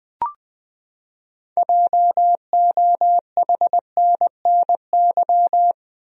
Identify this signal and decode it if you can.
Morse code, heard as a keyed tone, about 20 wpm, 700 Hz. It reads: JOHNNY
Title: USA Common Names - 20wpm